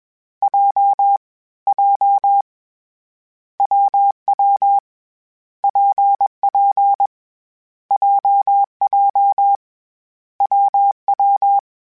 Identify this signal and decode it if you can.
Morse code, heard as a keyed tone, about 21 wpm, 800 Hz. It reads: J J WW PP JJ WW
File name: Variable Speed / Variable Speed-mp3s / WPJ WPJ1 sp @20